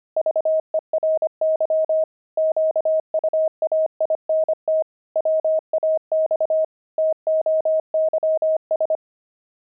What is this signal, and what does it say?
Morse code, heard as a keyed tone, about 25 wpm, 630 Hz. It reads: VERY QUAINT WAX TOYS
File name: pangram-fs-25-cs-25-toys